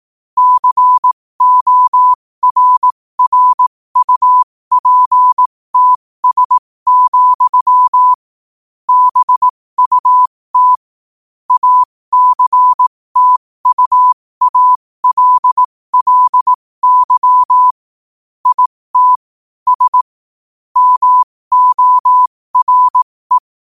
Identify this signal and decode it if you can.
Morse code, heard as a keyed tone, about 18 wpm, spaced as for 15 wpm, 1000 Hz. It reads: CORRUPTS, BUT ACTUALLY IT S MORE
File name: qotdfiles / 15wpm